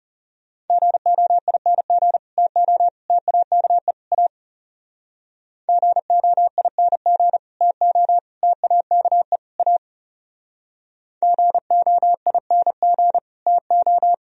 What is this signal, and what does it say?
Morse code, heard as a keyed tone, about 38 wpm, 700 Hz. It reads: GOING TO TAKE A GOING TO TAKE A GOING TO